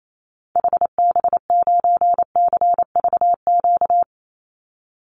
Morse code, heard as a keyed tone, about 28 wpm, 700 Hz. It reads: HB9CVQ
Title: Call Signs Repeated 2x - 28wpm